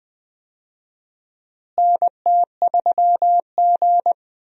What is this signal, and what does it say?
Morse code, heard as a keyed tone, about 20 wpm, 700 Hz. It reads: NT3G
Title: USA Call Signs - 4 Characters - Repeated 2x - 20wpm